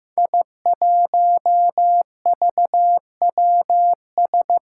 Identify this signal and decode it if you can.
Morse code, heard as a keyed tone, about 15 wpm, 700 Hz. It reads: I1VWS
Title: Five Challenging Letter-Number Combinations 15wpm